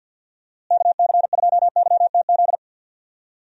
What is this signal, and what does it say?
Morse code, heard as a keyed tone, about 51 wpm, 700 Hz. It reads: KC2YTZ
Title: USA Call Signs - 6 Characters - Repeated 2x - 50wpm